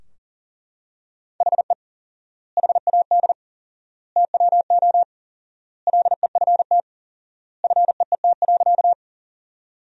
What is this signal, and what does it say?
Morse code, heard as a keyed tone, about 40 wpm, 700 Hz. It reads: HE HAD TWO LEFT FEET.
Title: 4x Word-Spacing - Sentences from Top 300 Words - 40wpm